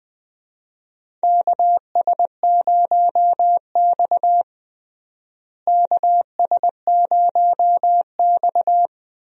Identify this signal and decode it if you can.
Morse code, heard as a keyed tone, about 20 wpm, 700 Hz. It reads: KS0X KS0X
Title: USA Call Signs - 4 Characters - Repeated 2x - 20wpm